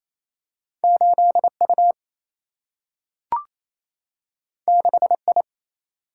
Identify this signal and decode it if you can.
Morse code, heard as a keyed tone, about 28 wpm, 700 Hz. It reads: 8 U 6 I
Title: Single-Letter Single-Number 28wpm